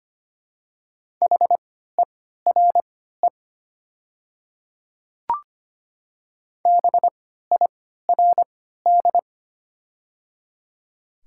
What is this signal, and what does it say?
Morse code, heard as a keyed tone, about 25 wpm, spaced as for 14 wpm, 690 Hz. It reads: HERE BIRD
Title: 3x Character Spacing - Top 500 Words - 25wpm